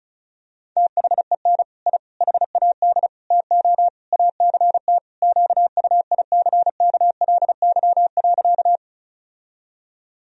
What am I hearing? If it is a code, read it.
Morse code, heard as a keyed tone, about 34 wpm, 700 Hz. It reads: THEN I HAD TO ACT QUICKLY.
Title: Sentences from Top 1000 Words - Review Part 1 - 35wpm